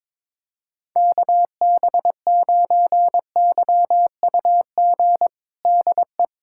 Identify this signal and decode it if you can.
Morse code, heard as a keyed tone, about 22 wpm, 700 Hz. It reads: KB9YUG DE
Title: USA Callsign DE Callsign - 6 Characters - 22wpm